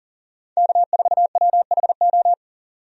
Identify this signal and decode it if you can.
Morse code, heard as a keyed tone, about 39 wpm, 700 Hz. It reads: K4WHO
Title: USA Call Signs - 5 Characters - 40wpm